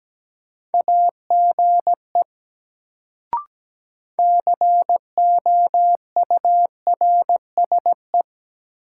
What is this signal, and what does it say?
Morse code, heard as a keyed tone, about 17 wpm, 700 Hz. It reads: AGE COURSE
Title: Sentences from Top 500 Words - New Word Drill - 17wpm